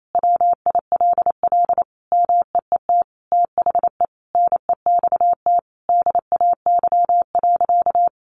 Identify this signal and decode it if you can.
Morse code, heard as a keyed tone, about 28 wpm, 700 Hz. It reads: WILL MEET THE NEXT DAY.
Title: Sentences from Top 1400 Words 28wpm